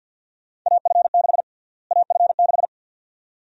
Morse code, heard as a keyed tone, about 51 wpm, 700 Hz. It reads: AF6 AF6
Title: USA Call Sign Prefixes - Repeated 2x - 50wpm